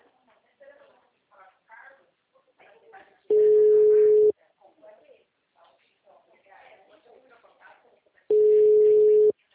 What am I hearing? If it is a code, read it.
Ringback tone (Europe, 425 Hz), with speech.